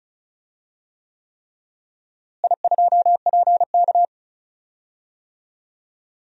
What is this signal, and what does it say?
Morse code, heard as a keyed tone, about 34 wpm, 700 Hz. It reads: I2PK
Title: DXCC-100 Call Signs - 4 Characters - Repeated 3x - 35wpm